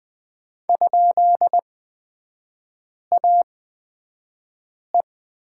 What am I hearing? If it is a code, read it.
Morse code, heard as a keyed tone, about 20 wpm, spaced as for 5 wpm, 700 Hz. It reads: ?AE
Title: 236 Lesson 38e - Letter q - Three Characters 20wpm